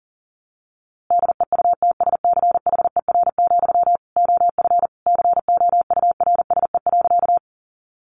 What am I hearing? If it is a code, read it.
Morse code, heard as a keyed tone, about 37 wpm, 700 Hz. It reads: DEUTSCHER, OF COURSE.
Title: A Sound Of Thunder 40wpm